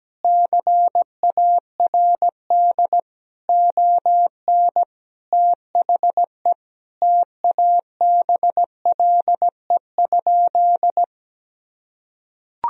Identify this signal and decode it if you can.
Morse code, heard as a keyed tone, about 17 wpm, 700 Hz. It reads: CARD ON THE TABLE?
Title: Sentences from Top 1000 Words - Review Part 4 - 17wpm